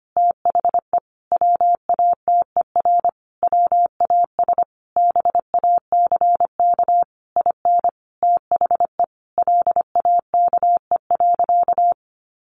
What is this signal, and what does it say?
Morse code, heard as a keyed tone, about 25 wpm, 700 Hz. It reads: THE WATER WAS BACK IN THE LAKE.
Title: Sentences from Top 1100 Words 25wpm